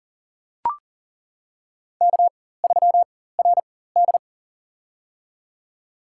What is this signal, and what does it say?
Morse code, heard as a keyed tone, about 40 wpm, spaced as for 19 wpm, 700 Hz. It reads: K3RD
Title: USA Call Signs - 4 Characters - 4x Character Spacing - 40wpm